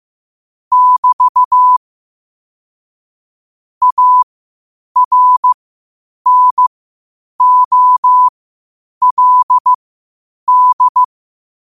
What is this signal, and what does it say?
Morse code, heard as a keyed tone, about 15 wpm, spaced as for 8 wpm, 1000 Hz. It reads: <BT> ARNOLD
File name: qotdfiles / 8wpm